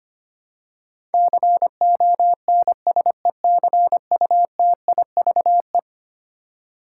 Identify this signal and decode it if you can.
Morse code, heard as a keyed tone, about 25 wpm, 700 Hz. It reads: CONSECUTIVE
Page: morse-code-ninja-practice-25wpm